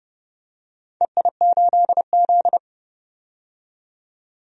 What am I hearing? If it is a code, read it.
Morse code, heard as a keyed tone, about 30 wpm, 700 Hz. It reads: EI8Z
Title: DXCC-100 Call Signs - 4 Characters - 30wpm